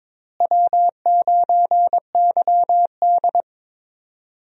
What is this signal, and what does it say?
Morse code, heard as a keyed tone, about 22 wpm, 700 Hz. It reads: W9YD
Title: USA Call Signs - 4 Characters - Repeated 3x - 22wpm